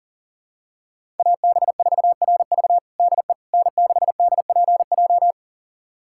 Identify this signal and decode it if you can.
Morse code, heard as a keyed tone, about 40 wpm, 700 Hz. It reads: AB4RV DE N6DPJ
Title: USA Callsign DE Callsign - 5 Characters - 40wpm